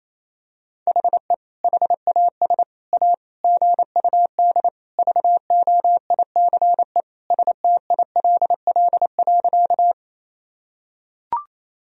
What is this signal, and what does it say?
Morse code, heard as a keyed tone, about 28 wpm, 700 Hz. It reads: HE HAS A GUD VOICE STILL.
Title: Sentences from Top 500 Words and Abbreviations 28wpm